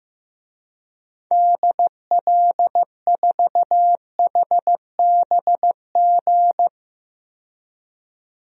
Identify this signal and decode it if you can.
Morse code, heard as a keyed tone, about 15 wpm, 700 Hz. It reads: DL4HBG
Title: DXCC-100 Call Signs - 6 Characters - 15wpm